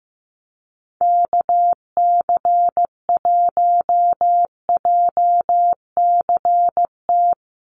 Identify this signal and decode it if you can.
Morse code, heard as a keyed tone, about 15 wpm, 700 Hz. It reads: KC1JCT
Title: USA Call Signs 15wpm